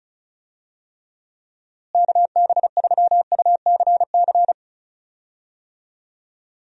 Morse code, heard as a keyed tone, about 34 wpm, 700 Hz. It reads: KB3UCC
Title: USA Call Signs - 6 Characters - Repeated 3x - 35wpm